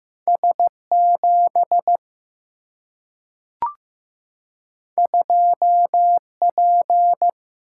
Morse code, heard as a keyed tone, about 15 wpm, 700 Hz. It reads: S7 2P
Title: Single-Letter Single-Number 15wpm